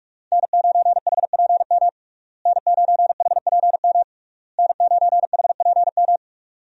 Morse code, heard as a keyed tone, about 46 wpm, 700 Hz. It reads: N9HPM N9HPM N9HPM
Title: USA Call Signs - 5 Characters - Repeated 3x - 45wpm